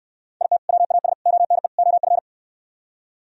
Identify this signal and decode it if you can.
Morse code, heard as a keyed tone, about 72 wpm, 700 Hz. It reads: IT WAS ONE OF